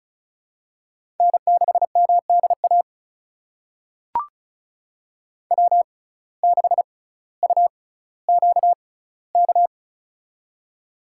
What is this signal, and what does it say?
Morse code, heard as a keyed tone, about 36 wpm, spaced as for 18 wpm, 700 Hz. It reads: N6MDA W6UQK
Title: USA Call Signs - 5 Characters - 6x Character Spacing - 35wpm